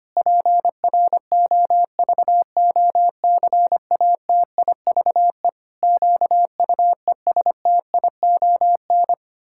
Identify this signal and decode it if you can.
Morse code, heard as a keyed tone, about 25 wpm, 700 Hz. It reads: PROVOCATIVE QUESTION
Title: Pangram Challenge v2 - 25wpm